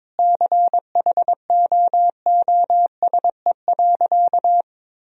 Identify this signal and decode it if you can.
Morse code, heard as a keyed tone, about 22 wpm, 700 Hz. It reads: CHOOSE.
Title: Sentences from Top 1000 Words - Review Part 4 - 22wpm